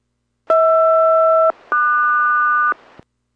Aircraft SELCAL, heard as two pairs of tones. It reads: HR-QS